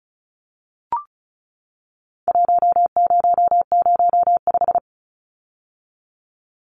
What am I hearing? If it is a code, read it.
Morse code, heard as a keyed tone, about 34 wpm, 700 Hz. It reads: 1005
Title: Numbers - Contest Style - 4-Digit 35wpm